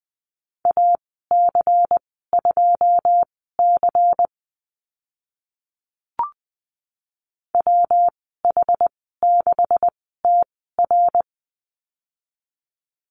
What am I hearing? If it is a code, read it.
Morse code, heard as a keyed tone, about 19 wpm, spaced as for 14 wpm, 700 Hz. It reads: AC2C WH6TR
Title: 140 Lesson 23g - Letter f - Call Signs 20wpm